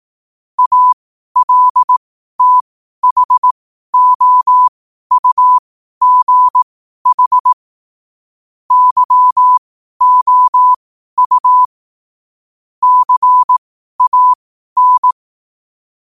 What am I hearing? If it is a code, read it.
Morse code, heard as a keyed tone, about 18 wpm, spaced as for 12 wpm, 1000 Hz. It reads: ALTHOUGH YOU CAN
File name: qotdfiles / 12wpm